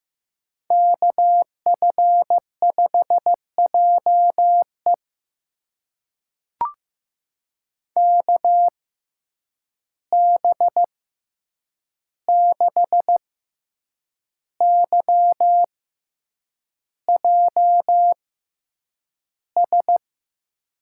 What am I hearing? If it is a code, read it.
Morse code, heard as a keyed tone, about 15 wpm, spaced as for 7 wpm, 700 Hz. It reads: KF5JE KB6YJS
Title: USA Call Signs - 6x Character Spacing - 15wpm